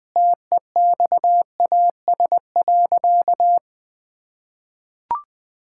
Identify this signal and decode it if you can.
Morse code, heard as a keyed tone, about 20 wpm, 700 Hz. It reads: TEXAS.
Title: Sentences with 4-Word Phrases - 20wpm